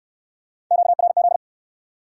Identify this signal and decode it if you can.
Morse code, heard as a keyed tone, about 68 wpm, 700 Hz. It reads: QRZ